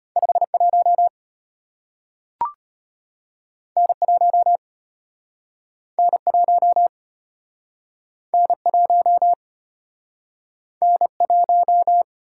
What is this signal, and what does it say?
Morse code, heard as a keyed tone, about 34 wpm, 700 Hz. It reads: 51 N1 N1 N1 N1